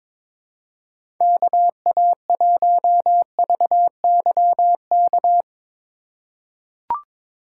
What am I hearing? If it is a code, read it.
Morse code, heard as a keyed tone, about 21 wpm, 700 Hz. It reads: KA1VYK
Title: USA Call Signs - 2x Character Spacing - 22wpm